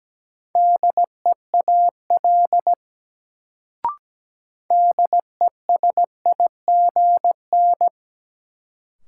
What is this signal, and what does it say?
Morse code, heard as a keyed tone, about 17 wpm, 700 Hz. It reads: DEAL DESIGN
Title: Sentences from Top 800 Words - New Word Drill - 17wpm